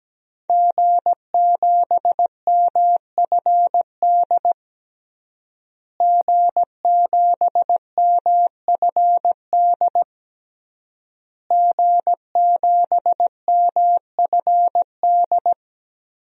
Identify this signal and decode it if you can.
Morse code, heard as a keyed tone, about 17 wpm, 700 Hz. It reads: G7MFD G7MFD G7MFD
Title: DXCC-100 Call Signs - 5 Characters - Repeated 3x - 17wpm